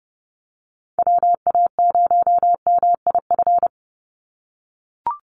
Morse code, heard as a keyed tone, about 30 wpm, 700 Hz. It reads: WA0MIF
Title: USA Call Signs 30wpm